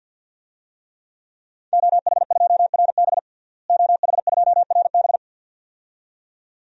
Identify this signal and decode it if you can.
Morse code, heard as a keyed tone, about 50 wpm, 700 Hz. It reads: OH2RB OH2RB
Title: DXCC-100 Call Signs - 5 Characters - Repeated 2x - 50wpm